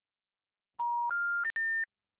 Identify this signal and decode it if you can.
Special information tone.